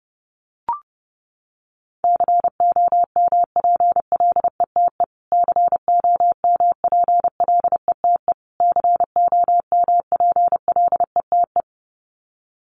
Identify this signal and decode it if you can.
Morse code, heard as a keyed tone, about 30 wpm, 700 Hz. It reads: COMPLETE COMPLETE COMPLETE
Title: Top 500 Words Repeated 30wpm